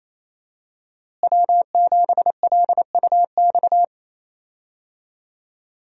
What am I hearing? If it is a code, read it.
Morse code, heard as a keyed tone, about 28 wpm, 700 Hz. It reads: W7LUX
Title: USA Call Signs - 5 Characters - Repeated 3x - 28wpm